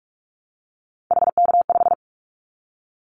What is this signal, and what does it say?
Morse code, heard as a keyed tone, about 44 wpm, 700 Hz. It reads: HK5
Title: Callsign Prefixes for Top 100 DXCC - 45wpm